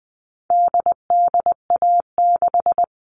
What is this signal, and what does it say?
Morse code, heard as a keyed tone, about 20 wpm, 700 Hz. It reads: DDA6
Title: All Cut Numbers - 4-Digit 20wpm